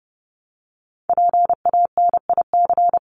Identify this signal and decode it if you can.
Morse code, heard as a keyed tone, about 30 wpm, 700 Hz. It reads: PANIC
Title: Top 1751-2000 Words 30wpm